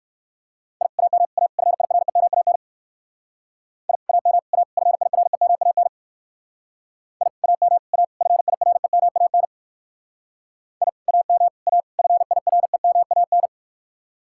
Morse code, heard as a keyed tone, about 58 wpm, 700 Hz. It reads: I AM A FIREMAN I AM A FIREMAN I AM A FIREMAN I AM A FIREMAN